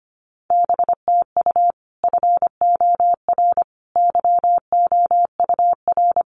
Morse code, heard as a keyed tone, about 25 wpm, 700 Hz. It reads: BTU FOR YOUR
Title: Common QSO Phrases 25wpm